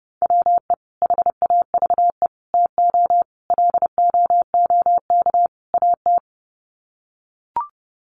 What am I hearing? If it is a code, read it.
Morse code, heard as a keyed tone, about 30 wpm, 700 Hz. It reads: WE HAVE TO LOOK AT
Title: Sets of 5 Words Encore2 30wpm